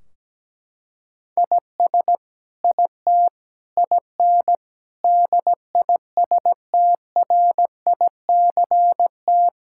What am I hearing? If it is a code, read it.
Morse code, heard as a keyed tone, about 17 wpm, 700 Hz. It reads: IS IT IN DISTRICT
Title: Sentences from Top 1600 Words 17wpm